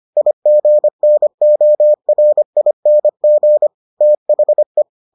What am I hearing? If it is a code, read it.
Morse code, heard as a keyed tone, about 25 wpm, 600 Hz. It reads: IGNORING THE